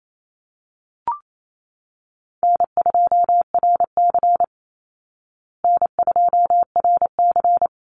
Morse code, heard as a keyed tone, about 28 wpm, 700 Hz. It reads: N2RC N2RC
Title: Easy Call Signs Repeated 2x - 28wpm